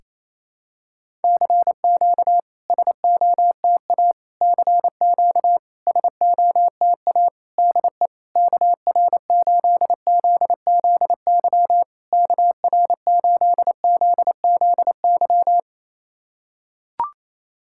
Morse code, heard as a keyed tone, about 29 wpm, 700 Hz. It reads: CQ SOTA CQ SOTA DE KR8ZZY KR8ZZY
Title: SOTA QSO Practice 28wpm